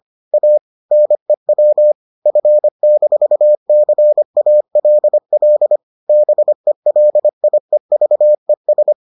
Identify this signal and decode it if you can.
Morse code, heard as a keyed tone, about 25 wpm, 600 Hz. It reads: A NEW F-CALL BELIEVES